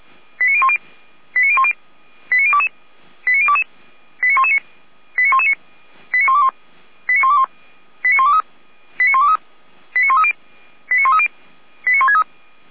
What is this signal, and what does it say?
Five-tone selective call (ZVEI), heard as quick runs of five tones, tones of about 70 ms. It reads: 89010 89010 89022 89022 89119 89119 89121 89121 89123 89123 89130 89130 89163
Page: ZVEI